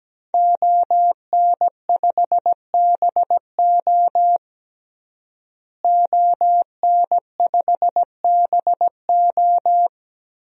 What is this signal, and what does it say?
Morse code, heard as a keyed tone, about 17 wpm, 700 Hz. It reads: ON5BO ON5BO
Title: DXCC-100 Call Signs - 5 Characters - Repeated 2x - 17wpm